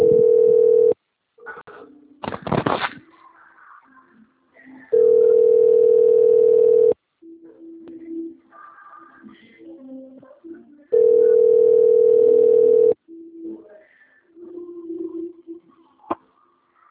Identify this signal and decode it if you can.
Ringback tone (North America), 440 + 480 Hz.
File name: out- - - - - .22476